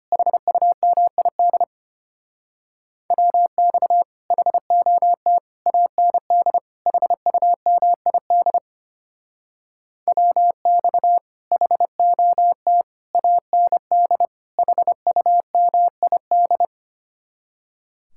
Morse code, heard as a keyed tone, about 32 wpm, 700 Hz. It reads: HUMID WX HOT AND HUMID WX HOT AND HUMID